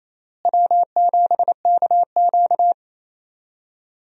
Morse code, heard as a keyed tone, about 28 wpm, 700 Hz. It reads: W7KQ
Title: USA Call Signs - 4 Characters - Repeated 3x - 28wpm